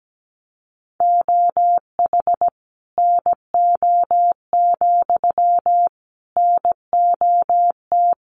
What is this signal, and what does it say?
Morse code, heard as a keyed tone, about 17 wpm, 700 Hz. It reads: OH NO, NOT
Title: Sentences from Top 1300 Words 17wpm